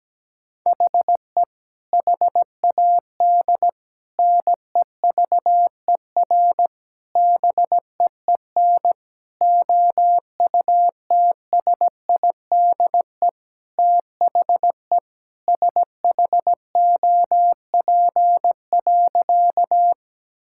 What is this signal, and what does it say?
Morse code, heard as a keyed tone, about 17 wpm, 700 Hz. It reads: HE HAD NEVER BEEN OUTSIDE THE SHOP.
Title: Sentences from Top 1600 Words 17wpm